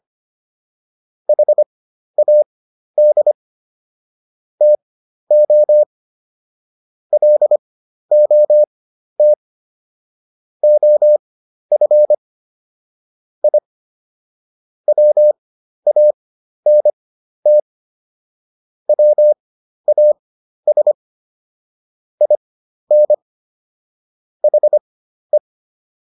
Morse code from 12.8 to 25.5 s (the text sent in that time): I WANT WAS IN HE